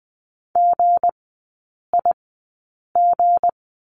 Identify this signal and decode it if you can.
Morse code, heard as a keyed tone, about 20 wpm, spaced as for 8 wpm, 690 Hz. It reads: GIG
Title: 159 Lesson 26e - Letter g - Three Characters 20wpm